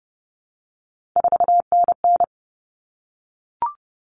Morse code, heard as a keyed tone, about 30 wpm, 700 Hz. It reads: 4NN
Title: All Cut Numbers - 3-Digit 30wpm